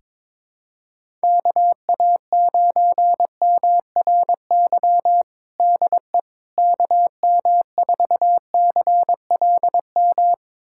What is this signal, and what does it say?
Morse code, heard as a keyed tone, about 22 wpm, 700 Hz. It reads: KA9MRY DE KM4CLM
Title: USA Callsign DE Callsign - 6 Characters - 22wpm